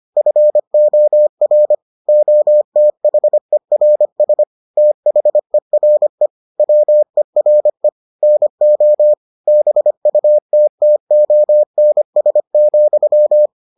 Morse code, heard as a keyed tone, about 25 wpm, 600 Hz. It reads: FOR OTHERS THERE WERE NO BUTTONS,